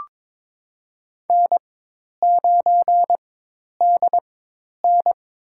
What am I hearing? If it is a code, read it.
Morse code, heard as a keyed tone, about 23 wpm, spaced as for 10 wpm, 700 Hz. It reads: N9DN
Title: USA Call Signs - 4 Characters - 4x Character Spacing - 22wpm